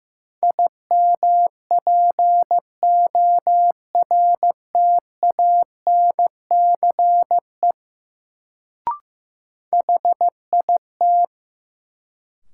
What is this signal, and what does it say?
Morse code, heard as a keyed tone, about 15 wpm, 700 Hz. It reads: IMPORTANCE HIT
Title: Sentences from Top 1200 Words - New Word Drill - 15wpm